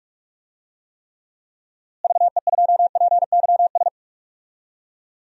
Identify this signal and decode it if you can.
Morse code, heard as a keyed tone, about 46 wpm, 700 Hz. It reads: VE2PYS
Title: DXCC-100 Call Signs - 6 Characters - Repeated 3x - 45wpm